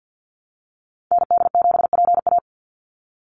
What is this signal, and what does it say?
Morse code, heard as a keyed tone, about 50 wpm, 700 Hz. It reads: ND7PA
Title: NA SOTA Activator Callsigns 50wpm